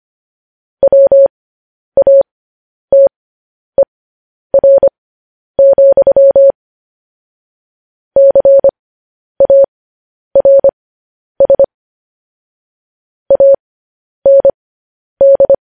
Morse code, heard as a keyed tone, about 25 wpm, spaced as for 10 wpm, 550 Hz. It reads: WATER, CARS AND
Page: News Headlines in Morse Code at 10 WPM